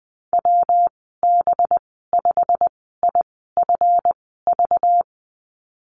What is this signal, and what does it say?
Morse code, heard as a keyed tone, about 20 wpm, spaced as for 14 wpm, 700 Hz. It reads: WB5IFV
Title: 193 Lesson 31g - Letter v - Call Signs 20wpm